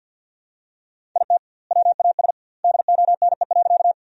Morse code, heard as a keyed tone, about 50 wpm, 700 Hz. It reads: IT WAS DONE.
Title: 2x Word-Spacing - Sentences from Top 500 Words - 50wpm